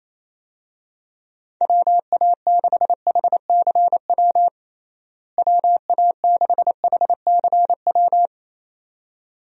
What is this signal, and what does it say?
Morse code, heard as a keyed tone, about 27 wpm, 700 Hz. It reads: WA6HCW WA6HCW
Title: USA Call Signs - 6 Characters - Repeated 2x - 28wpm